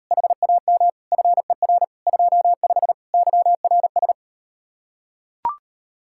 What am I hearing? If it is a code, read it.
Morse code, heard as a keyed tone, about 38 wpm, 700 Hz. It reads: HAM FER 25 YRS